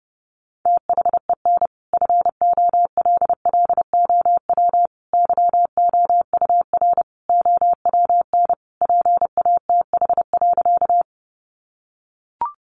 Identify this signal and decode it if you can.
Morse code, heard as a keyed tone, about 30 wpm, 700 Hz. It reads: THEN FOLLOW YOUR OWN PATH.
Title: Sentences from Top 1400 Words 30wpm